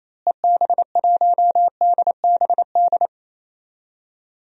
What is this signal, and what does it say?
Morse code, heard as a keyed tone, about 28 wpm, 700 Hz. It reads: EB1DBD
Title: DXCC-100 Call Signs - 6 Characters - Repeated 3x - 28wpm